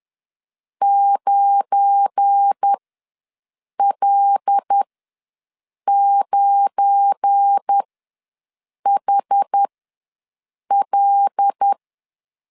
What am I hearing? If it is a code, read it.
Morse code, heard as a keyed tone, about 10 wpm, spaced as for 6 wpm, 800 Hz. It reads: 9L9HL
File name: LEZIONI / 7 GF9Z / 7 GF9 ws 10wpm 4034